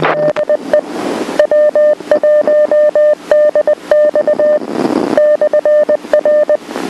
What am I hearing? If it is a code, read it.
Morse code, heard as a keyed tone, about 20 wpm, 600 Hz. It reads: DE W1DX /R
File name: TipsNet 11-20-19 . Q signals and ham radio lingo